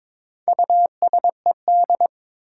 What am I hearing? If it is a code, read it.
Morse code, heard as a keyed tone, about 22 wpm, 700 Hz. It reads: USED
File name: Build A Phrase - 2-Word Phrases - 22wpm